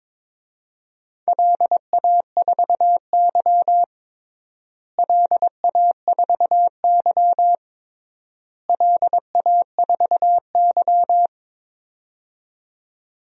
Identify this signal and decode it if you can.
Morse code, heard as a keyed tone, about 22 wpm, 700 Hz. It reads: LA4Y LA4Y LA4Y
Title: DXCC-100 Call Signs - 4 Characters - Repeated 3x - 22wpm